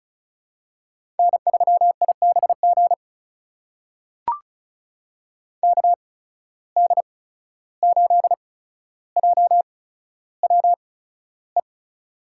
Morse code, heard as a keyed tone, about 36 wpm, spaced as for 14 wpm, 700 Hz. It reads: N3IBG KD8JWE
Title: USA Call Signs - 8x Character Spacing - 35wpm